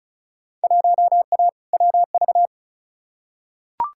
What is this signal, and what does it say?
Morse code, heard as a keyed tone, about 36 wpm, 700 Hz. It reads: 1A WV
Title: Field Day Exchanges Encore - 35wpm